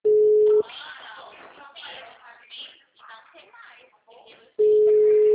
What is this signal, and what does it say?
Ringback tone (Europe, 425 Hz), with speech.